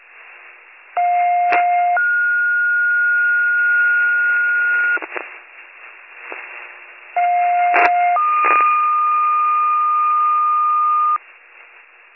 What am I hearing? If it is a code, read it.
Two-tone page, heard as a one-second tone, then a three-second tone, about 710 then 1430 Hz, about 710 then 1220 Hz, with noise. It is heard from 1.0 to 11.2 s.